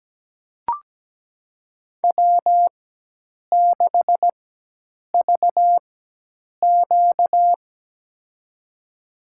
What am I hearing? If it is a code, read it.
Morse code, heard as a keyed tone, about 17 wpm, spaced as for 8 wpm, 700 Hz. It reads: W6VQ
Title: USA Call Signs - 4 Characters - 4x Character Spacing - 17wpm